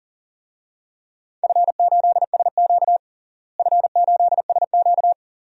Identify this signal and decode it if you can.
Morse code, heard as a keyed tone, about 37 wpm, 700 Hz. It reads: F8SQ F8SQ
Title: DXCC-100 Call Signs - 4 Characters - Repeated 2x - 40wpm